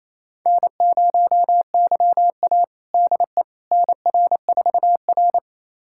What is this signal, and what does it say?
Morse code, heard as a keyed tone, about 28 wpm, 700 Hz. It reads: N0YA DE NR4R
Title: USA Callsign DE Callsign - 4 Characters - 28wpm